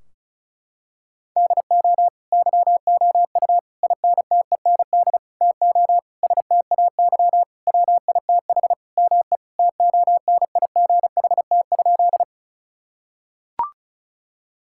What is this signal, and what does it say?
Morse code, heard as a keyed tone, about 36 wpm, 700 Hz. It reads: DO YOU INTEND TO STAY WITH ME TONIGHT?
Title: Sentences from Top 1600 Words 35wpm